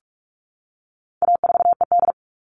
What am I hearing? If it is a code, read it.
Morse code, heard as a keyed tone, about 44 wpm, 700 Hz. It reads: A4ED
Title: All Cut Numbers - 4-Digit 45wpm